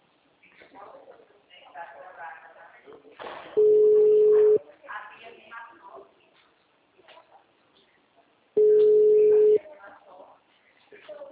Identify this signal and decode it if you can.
Ringback tone (Europe, 425 Hz), with speech.
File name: out- -2008